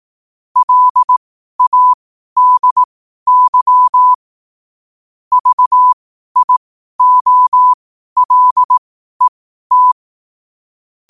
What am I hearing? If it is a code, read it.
Morse code, heard as a keyed tone, about 18 wpm, spaced as for 12 wpm, 1000 Hz. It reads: LADY VIOLET
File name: qotdfiles / 12wpm